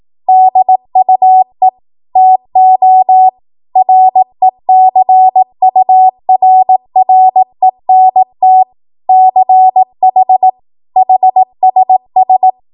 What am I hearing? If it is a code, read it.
Morse code, heard as a keyed tone, about 18 wpm, 750 Hz. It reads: DUE TO RECURRENT CH HSS
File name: Morse / 18WPM